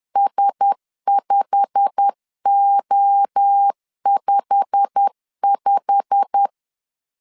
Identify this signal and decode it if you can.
Morse code, heard as a keyed tone, about 10 wpm, 800 Hz. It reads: S5O55
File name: LEZIONI / 8 YJ45 / 8 YJ45 10wpm 2549